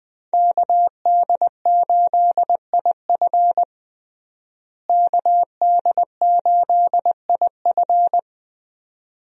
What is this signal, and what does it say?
Morse code, heard as a keyed tone, about 21 wpm, 700 Hz. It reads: KD8IF KD8IF
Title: USA Call Signs - 5 Characters - Repeated 2x - 20wpm